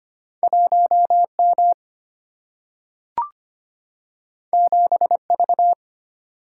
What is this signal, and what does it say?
Morse code, heard as a keyed tone, about 25 wpm, 700 Hz. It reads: 1M 7V
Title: Single-Letter Single-Number 25wpm